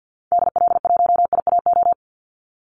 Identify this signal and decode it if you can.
Morse code, heard as a keyed tone, about 50 wpm, 700 Hz. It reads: DL1IAO
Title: Call Signs Repeated 2x - 50wpm